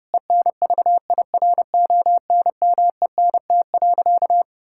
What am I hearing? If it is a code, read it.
Morse code, heard as a keyed tone, about 30 wpm, 700 Hz. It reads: ENVIRONMENT.
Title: Sentences from Top 1000 Words - Review Part 5 - 30wpm